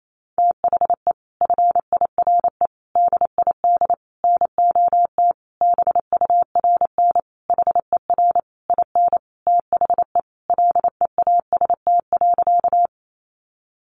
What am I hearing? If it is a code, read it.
Morse code, heard as a keyed tone, about 28 wpm, 700 Hz. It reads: THE FIRE DID NOT BURN HER IN THE LEAST.
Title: Sentences from Top 1300 Words 28wpm